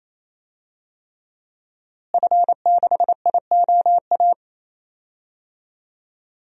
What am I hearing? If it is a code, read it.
Morse code, heard as a keyed tone, about 28 wpm, 700 Hz. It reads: F6IOA
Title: DXCC-100 Call Signs - 5 Characters - Repeated 3x - 28wpm